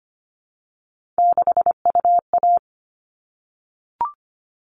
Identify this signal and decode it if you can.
Morse code, heard as a keyed tone, about 25 wpm, 700 Hz. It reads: 6UA
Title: All Cut Numbers - 3-Digit 25wpm